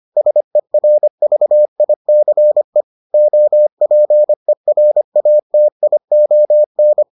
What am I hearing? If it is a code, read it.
Morse code, heard as a keyed tone, about 25 wpm, 600 Hz. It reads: SERVICE OPERATION